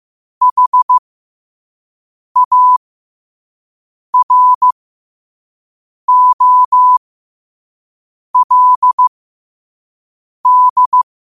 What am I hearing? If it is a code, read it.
Morse code, heard as a keyed tone, about 15 wpm, spaced as for 5 wpm, 1000 Hz. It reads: HAROLD
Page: Quotes for Tue, 12 Aug 2025 in Morse Code at 5 words per minute.